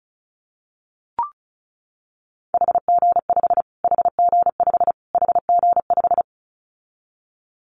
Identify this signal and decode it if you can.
Morse code, heard as a keyed tone, about 34 wpm, 700 Hz. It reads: HG5 HG5 HG5
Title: Callsign Prefixes for Top 25 DXCC - Repeated 3x - 35wpm